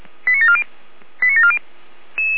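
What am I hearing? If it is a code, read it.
Five-tone selective call (ZVEI), heard as quick runs of five tones, tones of about 70 ms. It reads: 89740 89740
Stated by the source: ZVEI five-tone call received on SDR.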